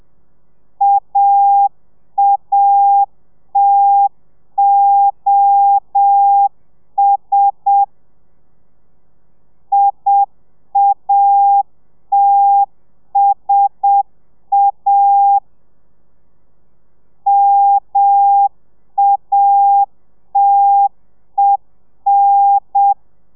Morse code, heard as a keyed tone, about 7 wpm, 800 Hz. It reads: AATOS IATSA MATEN